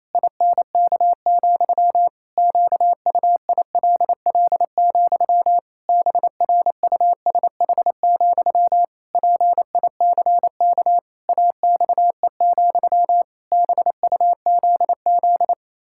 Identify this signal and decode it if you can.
Morse code, heard as a keyed tone, about 28 wpm, 700 Hz. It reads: INK, QUILL, BRUSH, PICK AXE, BUZZ
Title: Pangram Challenge v2 - 28wpm